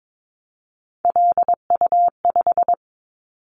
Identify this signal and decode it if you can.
Morse code, heard as a keyed tone, about 22 wpm, 700 Hz. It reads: LU5
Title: Callsign Prefixes for Top 100 DXCC - 22wpm